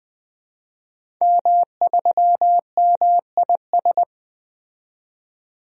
Morse code, heard as a keyed tone, about 21 wpm, 700 Hz. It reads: M3MIS
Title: DXCC-100 Call Signs - 5 Characters - 20wpm